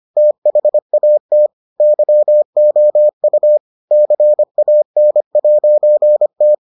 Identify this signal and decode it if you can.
Morse code, heard as a keyed tone, about 25 wpm, 600 Hz. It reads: THAT YOU CAN'T